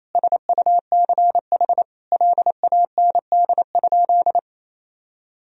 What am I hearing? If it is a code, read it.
Morse code, heard as a keyed tone, about 27 wpm, 700 Hz. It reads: SUCH LAND?
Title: Sentences from Top 1000 Words - Review Part 1 - 28wpm